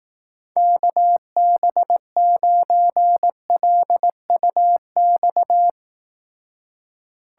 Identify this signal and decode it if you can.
Morse code, heard as a keyed tone, about 18 wpm, 700 Hz. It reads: KB9LUX
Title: Speed Racing - USA 2x3 Call Signs 15wpm